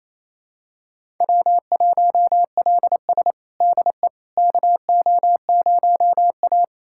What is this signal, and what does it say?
Morse code, heard as a keyed tone, about 28 wpm, 700 Hz. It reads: W1LS DE KO0A
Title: USA Callsign DE Callsign - 4 Characters - 28wpm